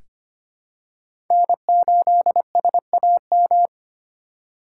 Morse code, heard as a keyed tone, about 25 wpm, 700 Hz. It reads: N8SAM
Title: USA Call Signs - 5 Characters - 2x Character Spacing - 25wpm